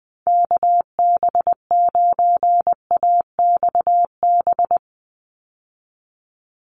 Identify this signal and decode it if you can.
Morse code, heard as a keyed tone, about 20 wpm, 700 Hz. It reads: KB9AXB
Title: USA Call Signs 20wpm